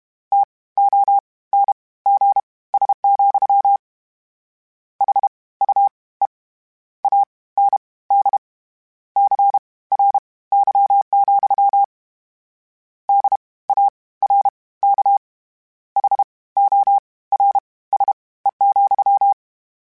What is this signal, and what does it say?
Morse code, heard as a keyed tone, about 32 wpm, spaced as for 19 wpm, 800 Hz. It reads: TONGS, HUE AND CRY, DARK HORSE,
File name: Misc / Misc-mp3s / WordPairs sp @30